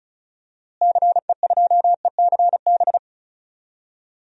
Morse code, heard as a keyed tone, about 34 wpm, 700 Hz. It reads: CE2ECB
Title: DXCC-100 Call Signs - 6 Characters - Repeated 3x - 35wpm